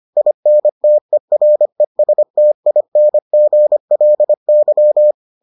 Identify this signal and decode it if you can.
Morse code, heard as a keyed tone, about 25 wpm, 600 Hz. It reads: INTERESTINGLY